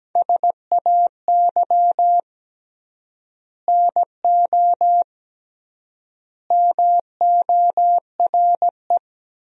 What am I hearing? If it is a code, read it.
Morse code, heard as a keyed tone, about 17 wpm, 700 Hz. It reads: SAY NO MORE
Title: 3x Word-Spacing - Sentences from Top 200 Words - 17wpm